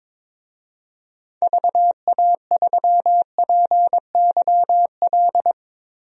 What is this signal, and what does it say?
Morse code, heard as a keyed tone, about 22 wpm, 700 Hz. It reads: VA3PYL
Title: DXCC-100 Call Signs - 6 Characters - Repeated 3x - 22wpm